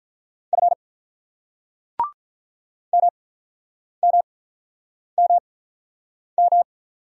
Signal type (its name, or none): Morse code